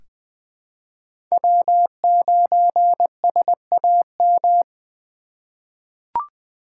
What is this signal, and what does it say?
Morse code, heard as a keyed tone, about 20 wpm, 700 Hz. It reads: W9SAM
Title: USA Call Signs - 5 Characters - 4x Character Spacing - 20wpm